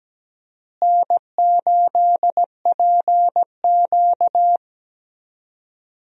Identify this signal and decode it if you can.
Morse code, heard as a keyed tone, about 17 wpm, 700 Hz. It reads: N8PQ
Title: USA Call Signs - 4 Characters - Repeated 2x - 17wpm